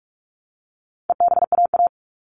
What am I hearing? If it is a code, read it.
Morse code, heard as a keyed tone, about 44 wpm, 700 Hz. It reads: EBAA
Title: All Cut Numbers - 4-Digit 45wpm